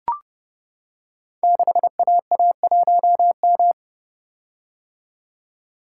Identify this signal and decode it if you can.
Morse code, heard as a keyed tone, about 30 wpm, 700 Hz. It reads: 6AA1M
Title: Code Groups - 5 Letter:Numbers - 30wpm